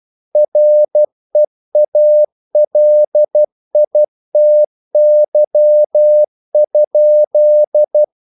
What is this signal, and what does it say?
Morse code, heard as a keyed tone, about 12 wpm, 600 Hz. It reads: REALITY?
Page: English quotations in CW (telegraphy - morse)